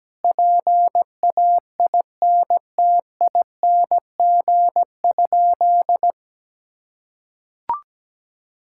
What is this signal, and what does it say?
Morse code, heard as a keyed tone, about 17 wpm, 700 Hz. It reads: PAINTING?
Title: Sentences from Top 1000 Words - Review Part 6 - 17wpm